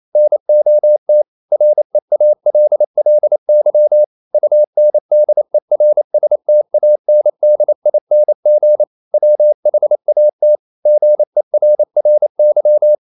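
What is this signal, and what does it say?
Morse code, heard as a keyed tone, about 28 wpm, 600 Hz. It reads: NOT REALLY UNDERSTANDING WHAT GERRY
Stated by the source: Morse code practice at 28 wpm, text of Ken Follett's novel Never in English, from CW med Gnister.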